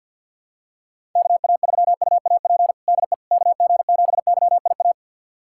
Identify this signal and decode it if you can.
Morse code, heard as a keyed tone, about 51 wpm, 700 Hz. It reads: KA3UAP DE KG7YIA
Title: USA Callsign DE Callsign - 6 Characters - 50wpm